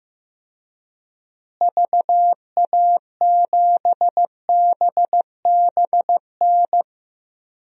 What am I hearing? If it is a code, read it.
Morse code, heard as a keyed tone, about 15 wpm, 700 Hz. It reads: VA7BBN
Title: DXCC-100 Call Signs - 6 Characters - Repeated 3x - 15wpm